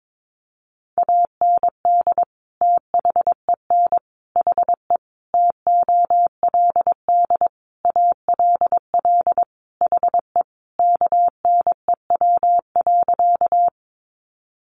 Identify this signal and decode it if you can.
Morse code, heard as a keyed tone, about 22 wpm, 700 Hz. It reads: AND THEN HE TOLD ALL HE KNEW.
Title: Sentences from Top 400 Words 22wpm